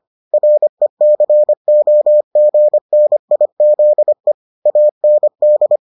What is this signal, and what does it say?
Morse code, heard as a keyed tone, about 25 wpm, 600 Hz. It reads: RECOGNIZE AND